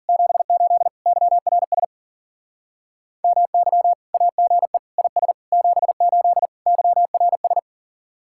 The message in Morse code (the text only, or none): 78 YRS MY AGE IS 78 YRS